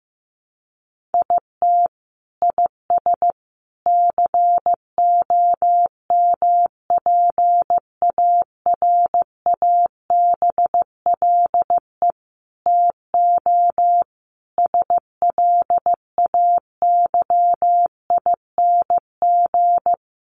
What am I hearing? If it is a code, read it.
Morse code, heard as a keyed tone, about 15 wpm, 700 Hz. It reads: IT IS COMPARABLE TO SLAYING
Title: A Sound Of Thunder 15wpm